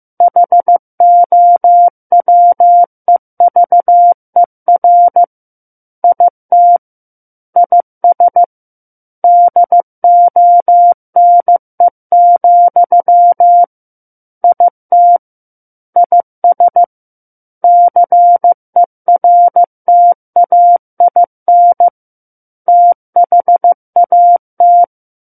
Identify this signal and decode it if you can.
Morse code, heard as a keyed tone, about 15 wpm, 700 Hz. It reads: HOWEVER IT IS DONE, IT IS CERTAIN THAT
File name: Chapter 6 - 15 WPM